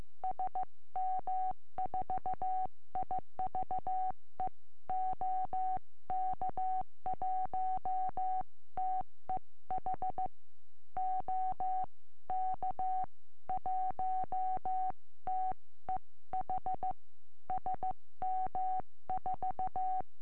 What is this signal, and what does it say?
Morse code, heard as a keyed tone, about 14 wpm, 750 Hz. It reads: SM4IVE OK1TEH OK1TEH SM4
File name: eme / sm4ive 70cm